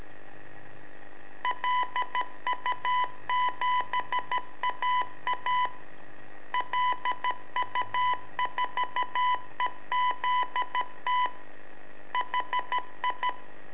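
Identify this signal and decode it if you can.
Morse code, heard as a keyed tone, about 13 wpm, 1000 Hz. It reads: LU7AA LU4EZT HI
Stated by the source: Telemetry in Morse from LU7AA and LU4EZ, AMSAT and Turma Mar del Plata, sent from a balloon.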